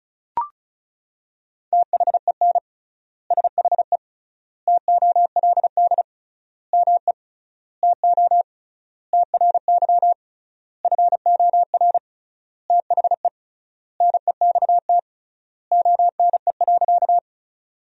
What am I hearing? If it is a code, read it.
Morse code, heard as a keyed tone, about 36 wpm, 700 Hz. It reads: THEN SHE TOLD ME TO TRY FOR THE NEXT ONE.
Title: 3x Word-Spacing - Sentences from Top 400 Words - 35wpm